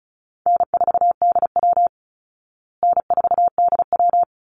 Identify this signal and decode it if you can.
Morse code, heard as a keyed tone, about 34 wpm, 700 Hz. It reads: N4DW N4DW
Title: Easy Call Signs Repeated 3x - 35wpm